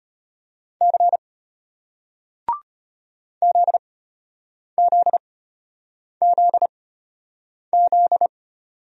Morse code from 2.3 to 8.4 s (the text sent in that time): ZZZZ